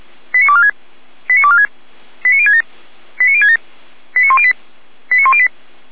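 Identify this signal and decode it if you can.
Five-tone selective call (ZVEI), heard as quick runs of five tones, tones of about 70 ms. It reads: 89246 89246 89067 89067 89108 89108
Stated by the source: ZVEI five-tone selective call received on SDR.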